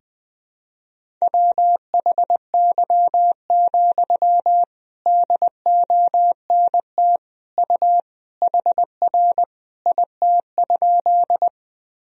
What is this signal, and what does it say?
Morse code, heard as a keyed tone, about 21 wpm, 700 Hz. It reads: WHY, DONT U HR IT?
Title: Sentences from Top 300 Words and Abbreviations 20wpm